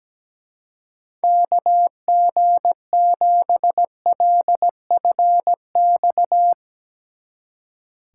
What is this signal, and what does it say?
Morse code, heard as a keyed tone, about 17 wpm, 700 Hz. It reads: KG7LFX
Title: Speed Racing - USA 2x3 Call Signs 17wpm